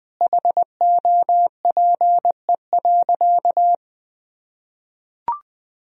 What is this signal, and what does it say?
Morse code, heard as a keyed tone, about 21 wpm, 700 Hz. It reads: HOPE.
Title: Sentences from Top 1000 Words - Review Part 3 - 20wpm